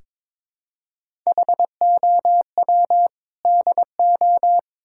Morse code, heard as a keyed tone, about 22 wpm, 700 Hz. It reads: HOW DO
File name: practice / 22wpm / Build A Phrase - 5-Word Phrases - 22wpm